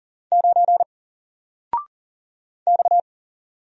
Morse code, heard as a keyed tone, about 40 wpm, 700 Hz. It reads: 9 X
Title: Single Letter-Number 40wpm